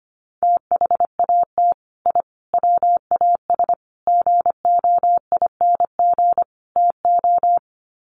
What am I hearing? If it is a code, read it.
Morse code, heard as a keyed tone, about 25 wpm, 700 Hz. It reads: THAT I WAS GOING TO
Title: Sets of 5 Words Encore 25wpm